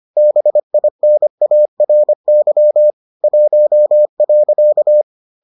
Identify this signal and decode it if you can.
Morse code, heard as a keyed tone, about 25 wpm, 600 Hz. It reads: BINARY 1.